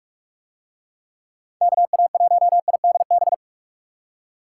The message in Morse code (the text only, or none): KA1IDB